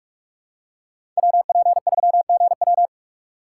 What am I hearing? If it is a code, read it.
Morse code, heard as a keyed tone, about 44 wpm, 700 Hz. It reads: WP3GW
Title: USA Call Signs - 5 Characters - 2x Character Spacing - 45wpm